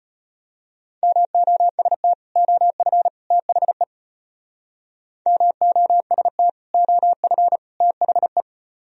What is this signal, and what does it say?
Morse code, heard as a keyed tone, about 32 wpm, 700 Hz. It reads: MOST OF THE MOST OF THE